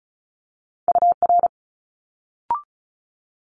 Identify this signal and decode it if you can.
Morse code, heard as a keyed tone, about 35 wpm, 700 Hz. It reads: UR
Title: Multi-letter Phonemes Quick Fire 35wpm